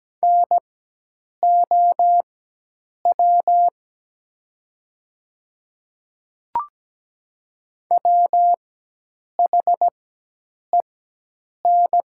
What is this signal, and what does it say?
Morse code, heard as a keyed tone, about 17 wpm, spaced as for 8 wpm, 700 Hz. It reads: NOW WHEN
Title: 4x Character Spacing - Top 100 Words - 17wpm